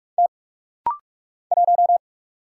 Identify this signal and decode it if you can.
Morse code, heard as a keyed tone, about 46 wpm, 700 Hz. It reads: T 1
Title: Single Letter-Number Alt - Rapid-Fire 45wpm